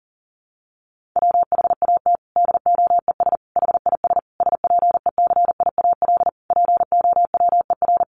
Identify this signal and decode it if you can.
Morse code, heard as a keyed tone, about 40 wpm, 700 Hz. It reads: WHAT DOES HIS SPECIAL POWER
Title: Sentences from Top 1400 Words 40wpm